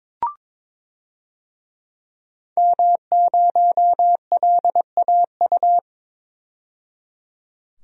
Morse code, heard as a keyed tone, about 21 wpm, 700 Hz. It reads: M0LAU
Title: DXCC-100 Call Signs - 5 Characters - 22wpm